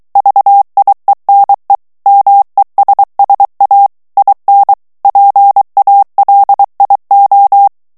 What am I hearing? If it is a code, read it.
Morse code, heard as a keyed tone, about 23 wpm, 800 Hz. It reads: VIENE MESSA IN PALIO
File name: ITALIEN texte clair 2 vitesse 22 mots minute